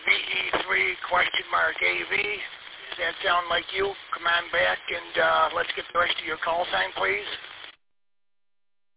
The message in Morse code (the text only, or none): none